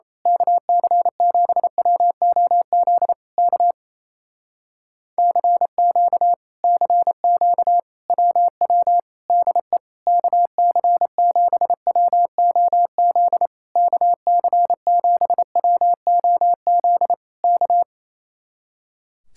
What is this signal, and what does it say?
Morse code, heard as a keyed tone, about 27 wpm, 700 Hz. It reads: KC7WOZ K CQ CQ WW DE KC7WOZ KC7WOZ K
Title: Speed-Racing Common QSO Phrases Encore - 28wpm